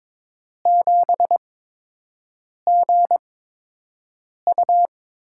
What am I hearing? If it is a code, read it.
Morse code, heard as a keyed tone, about 22 wpm, spaced as for 6 wpm, 700 Hz. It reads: 7GU
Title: USA Call Signs - 6 Characters - 8x Character Spacing - 22wpm